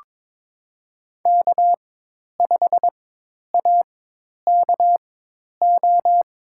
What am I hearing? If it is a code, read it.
Morse code, heard as a keyed tone, about 22 wpm, spaced as for 10 wpm, 700 Hz. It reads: K5AKO
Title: USA Call Signs - 5 Characters - 4x Character Spacing - 22wpm